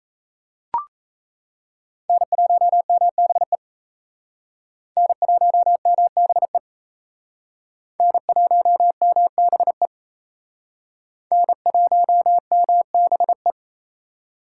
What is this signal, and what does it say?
Morse code, heard as a keyed tone, about 36 wpm, 700 Hz. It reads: N1MBE N1MBE N1MBE N1MBE